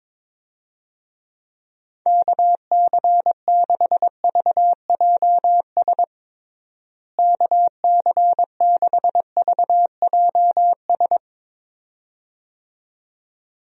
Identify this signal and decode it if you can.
Morse code, heard as a keyed tone, about 22 wpm, 700 Hz. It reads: KC6VJS KC6VJS
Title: USA Call Signs - 6 Characters - Repeated 2x - 22wpm